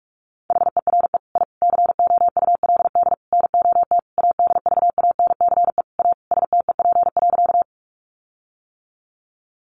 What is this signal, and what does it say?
Morse code, heard as a keyed tone, about 44 wpm, 700 Hz. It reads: HERE I COULD NOT ADVANCE A STEP.
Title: Sentences from Top 1400 Words 45wpm